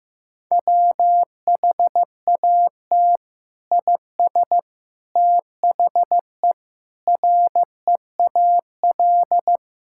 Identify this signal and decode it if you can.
Morse code, heard as a keyed tone, about 15 wpm, 700 Hz. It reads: WHAT IS THE REAL
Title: Sentences from Top 1000 Words - Review Part 6 - 15wpm